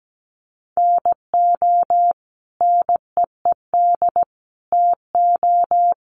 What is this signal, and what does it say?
Morse code, heard as a keyed tone, about 17 wpm, 700 Hz. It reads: NO NEED TO
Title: Sentences from Top 1100 Words 17wpm